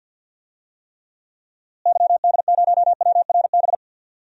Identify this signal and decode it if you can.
Morse code, heard as a keyed tone, about 51 wpm, 700 Hz. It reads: YD0WRB
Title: DXCC-100 Call Signs - 6 Characters - Repeated 3x - 50wpm